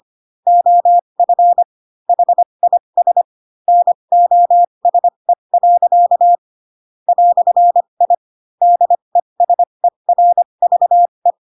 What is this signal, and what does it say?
Morse code, heard as a keyed tone, about 25 wpm, spaced as for 22 wpm, 700 Hz. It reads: OF HIS NOSE. "I DESERVE
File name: Pinocchio / Pinocchio ch3 18 25 ik1hge